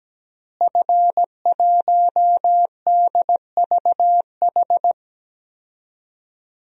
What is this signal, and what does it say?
Morse code, heard as a keyed tone, about 17 wpm, 700 Hz. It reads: F1DVH
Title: DXCC-100 Call Signs - 5 Characters - Repeated 2x - 17wpm